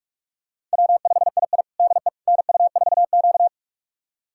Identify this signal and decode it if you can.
Morse code, heard as a keyed tone, about 41 wpm, 700 Hz. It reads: W5II DE NU4Q
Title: USA Callsign DE Callsign - 4 Characters - 45wpm